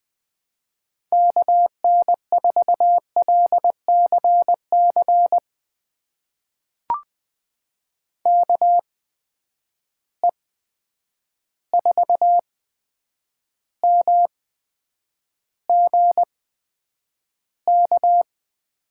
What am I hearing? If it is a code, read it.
Morse code, heard as a keyed tone, about 20 wpm, spaced as for 9 wpm, 700 Hz. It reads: KN4LCC KE4MGK
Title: USA Call Signs - 6 Characters - 8x Character Spacing - 20wpm